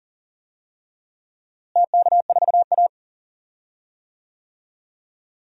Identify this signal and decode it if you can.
Morse code, heard as a keyed tone, about 39 wpm, 700 Hz. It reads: TK4A
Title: DXCC-100 Call Signs - 4 Characters - Repeated 2x - 40wpm